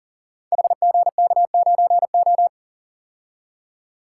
Morse code, heard as a keyed tone, about 40 wpm, 690 Hz. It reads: HGK9O
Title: Code Groups - 5 Letter:Numbers - 40wpm